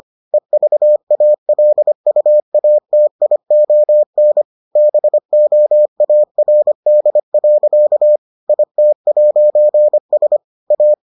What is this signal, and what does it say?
Morse code, heard as a keyed tone, about 25 wpm, 600 Hz. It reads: EVALUATION BOARD. IT'S A